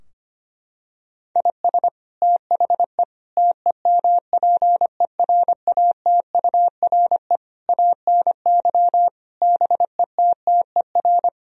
Morse code, heard as a keyed tone, about 25 wpm, 700 Hz. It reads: IS THE TEMPERATURE ANY BETTER
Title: Sentences from Top 1000 Words - Review Part 4 - 25wpm